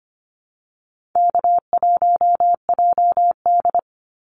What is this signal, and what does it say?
Morse code, heard as a keyed tone, about 25 wpm, 700 Hz. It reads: K1JD
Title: NA SOTA Activator Callsigns 25wpm